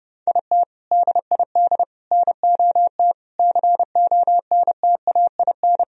Morse code, heard as a keyed tone, about 32 wpm, 700 Hz. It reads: IT DID NOT CONTAIN
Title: Sentences from Top 1000 Words - Review Part 2 - 30wpm